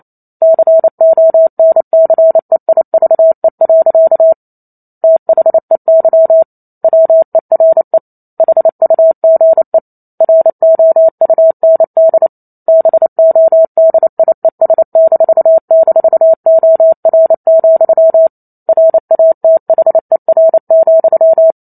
Morse code, heard as a keyed tone, about 28 wpm, 650 Hz. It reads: CONCEIVE. THEY WERE HUGE ROUND BODIES--OR, RATHER,
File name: Chapter 2 - 28 WPM